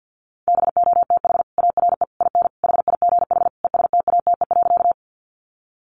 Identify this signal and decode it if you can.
Morse code, heard as a keyed tone, about 50 wpm, 700 Hz. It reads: BOTH ARE IN HIGH ESTATE.
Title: Sentences from Top 1200 Words 50wpm